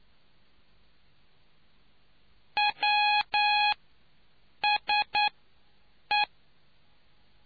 Morse code, heard as a keyed tone, about 9 wpm, spaced as for 6 wpm, 830 Hz. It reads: WSE